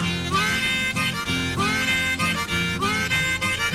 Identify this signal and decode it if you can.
Emergency Alert System (SAME) tones, heard as harsh buzzing bursts. It is heard from 0.4 to 3.5 s.
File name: wrct-log hi- 18